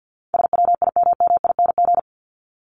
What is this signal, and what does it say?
Morse code, heard as a keyed tone, about 50 wpm, 700 Hz. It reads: SWIMMING
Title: Top 1501-1750 Words 50wpm